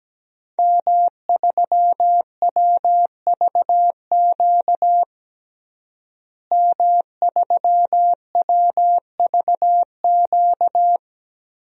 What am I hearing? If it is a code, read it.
Morse code, heard as a keyed tone, about 17 wpm, 700 Hz. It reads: M3WVQ M3WVQ
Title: DXCC-100 Call Signs - 5 Characters - Repeated 2x - 17wpm